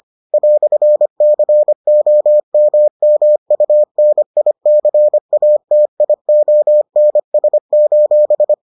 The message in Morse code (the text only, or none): "COMMUNICATIONS: